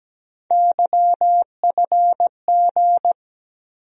Morse code, heard as a keyed tone, about 17 wpm, 700 Hz. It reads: YFG
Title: USA Call Sign Postfixes - 3 Letters 17wpm